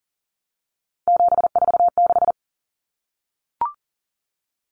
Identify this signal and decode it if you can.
Morse code, heard as a keyed tone, about 40 wpm, 700 Hz. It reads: 746
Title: Common Cut Numbers - 3-Digit 40wpm